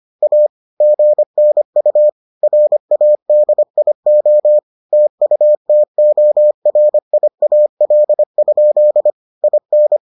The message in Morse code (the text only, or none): A GNU RADIO TUTORIAL? IN